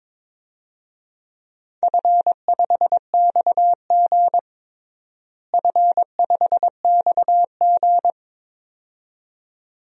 Morse code, heard as a keyed tone, about 22 wpm, 700 Hz. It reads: F5XG F5XG
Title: DXCC-100 Call Signs - 4 Characters - Repeated 2x - 22wpm